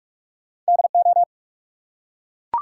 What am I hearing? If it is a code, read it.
Morse code, heard as a keyed tone, about 44 wpm, 700 Hz. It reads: DO
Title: Two Letter Words - 45wpm